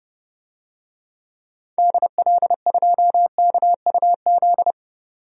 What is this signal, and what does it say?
Morse code, heard as a keyed tone, about 32 wpm, 700 Hz. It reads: DL2KUZ
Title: DXCC-100 Call Signs - 6 Characters - 30wpm